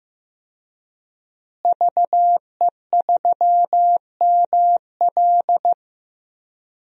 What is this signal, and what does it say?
Morse code, heard as a keyed tone, about 15 wpm, 700 Hz. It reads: VE3ML
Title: DXCC-100 Call Signs - 5 Characters - Repeated 3x - 15wpm